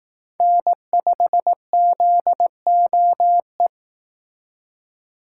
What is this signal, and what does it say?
Morse code, heard as a keyed tone, about 18 wpm, 700 Hz. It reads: N5ZOE
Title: Speed Racing - USA Call Signs 15wpm